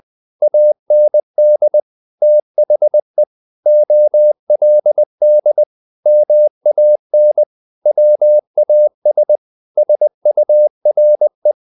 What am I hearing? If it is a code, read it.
Morse code, heard as a keyed tone, about 20 wpm, 600 Hz. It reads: AND THE OLD MAN WAS SURE
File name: the old man and the sea part 2 20wpm